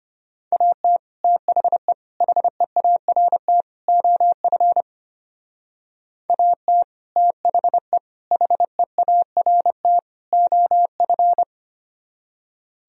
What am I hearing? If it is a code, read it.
Morse code, heard as a keyed tone, about 31 wpm, 700 Hz. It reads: AT THE HEART OF AT THE HEART OF
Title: Speed-Racing - Sets of 4 Words 25wpm